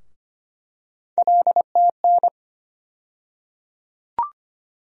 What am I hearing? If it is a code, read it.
Morse code, heard as a keyed tone, about 25 wpm, 700 Hz. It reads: LTN
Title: USA Call Sign Postfixes - 3 Letters 25wpm